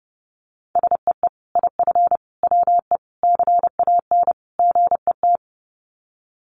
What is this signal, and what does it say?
Morse code, heard as a keyed tone, about 30 wpm, 700 Hz. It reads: SEE IF WE CAN GET
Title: Sets of 5 Words Encore2 30wpm